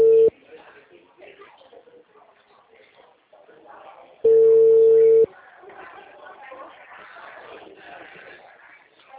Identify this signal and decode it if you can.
Ringback tone (China, 450 Hz), with speech.
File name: out- -2008